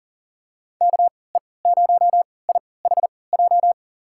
Morse code, heard as a keyed tone, about 40 wpm, spaced as for 23 wpm, 700 Hz. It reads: KE0IHJ
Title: USA Call Signs - 6 Characters - 3x Character Spacing - 40wpm